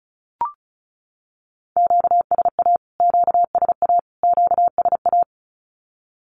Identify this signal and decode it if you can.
Morse code, heard as a keyed tone, about 34 wpm, 700 Hz. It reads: QSA QSA QSA
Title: Q-Codes Repeated 3x 35wpm